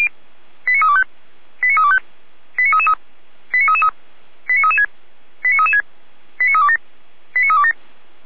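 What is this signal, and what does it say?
Five-tone selective call (ZVEI), heard as quick runs of five tones, tones of about 70 ms. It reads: 89325 89325 89333 89333 89336 89336 89327 89327
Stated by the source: SDR-received ZVEI five-tone selective call, address 89320.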